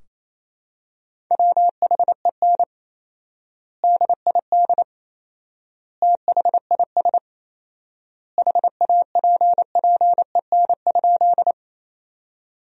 Morse code, heard as a keyed tone, about 28 wpm, 700 Hz. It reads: WHEN DID THIS HAPPEN?
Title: 4x Word-Spacing - Sentences from Top 400 Words - 28wpm